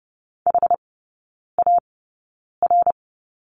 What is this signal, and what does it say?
Morse code, heard as a keyed tone, about 30 wpm, spaced as for 9 wpm, 700 Hz. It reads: HAR
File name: 081 Lesson 14e - Number 2 - Three Characters 30wpm